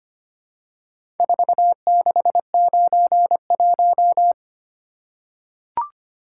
Morse code, heard as a keyed tone, about 25 wpm, 700 Hz. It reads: 4691
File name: practice / 25wpm / Four Letter-Number Combinations - 25wpm